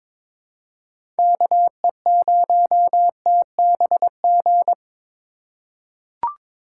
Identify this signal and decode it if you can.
Morse code, heard as a keyed tone, about 22 wpm, 700 Hz. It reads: KE0TBG
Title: USA Call Signs - 6 Characters - 3x Character Spacing - 22wpm